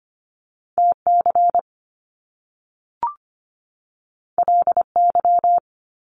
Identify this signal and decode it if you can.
Morse code, heard as a keyed tone, about 25 wpm, 700 Hz. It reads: TC LY
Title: Top Two Letter Combinations in Words 25wpm